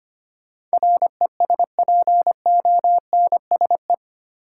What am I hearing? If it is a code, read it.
Morse code, heard as a keyed tone, about 25 wpm, 700 Hz. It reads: RESPONSE
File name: Eight Letter Words - 25wpm